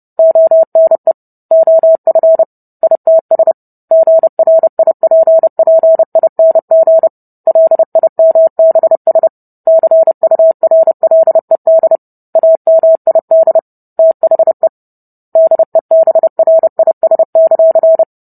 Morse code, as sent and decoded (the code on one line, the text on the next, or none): --- -. / --- ..-. / .. - ... / --. .-. .. .--. .--. .. -. --. / .-.. .. -- -... ... / -.-. ..- .-. .-.. -.. / .- -- .. -.. / - .... / -.. -... .-. .. ... -.-.-.
ONE OF ITS GRIPPING LIMBS CURLED AMID THE DEBRIS;